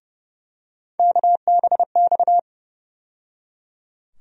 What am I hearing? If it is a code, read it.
Morse code, heard as a keyed tone, about 30 wpm, 700 Hz. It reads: KBX
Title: USA Call Sign Postfixes - 3 Letters 30wpm